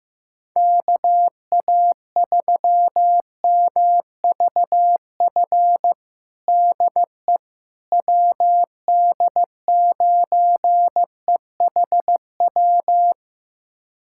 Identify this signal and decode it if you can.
Morse code, heard as a keyed tone, about 15 wpm, 700 Hz. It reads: KA3MVF DE WD9EHW
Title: USA Callsign DE Callsign 15wpm